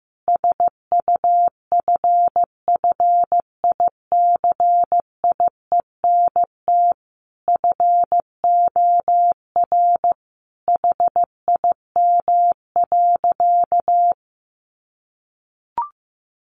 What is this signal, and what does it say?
Morse code, heard as a keyed tone, about 15 wpm, 700 Hz. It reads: SUFFICIENT FOR HIM.
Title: Sentences from Top 1400 Words 15wpm